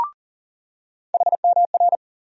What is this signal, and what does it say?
Morse code, heard as a keyed tone, about 40 wpm, 700 Hz. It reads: HMR
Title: USA Call Sign Postfixes - 3 Letters 40wpm